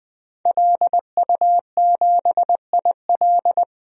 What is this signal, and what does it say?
Morse code, heard as a keyed tone, about 20 wpm, 700 Hz. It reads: LU7IL
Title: DXCC-100 Call Signs - 5 Characters - 20wpm